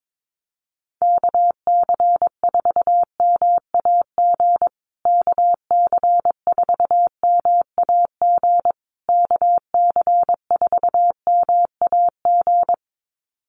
Morse code, heard as a keyed tone, about 22 wpm, 700 Hz. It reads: KC4MAG KC4MAG KC4MAG
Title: USA Call Signs Repeated 3x 22wpm